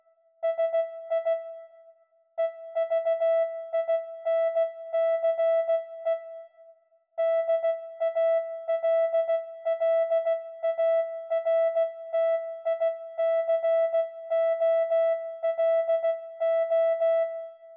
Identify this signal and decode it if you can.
Morse code, heard as a keyed tone, about 16 wpm, 660 Hz. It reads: SI EVINCE DALLARTICOLO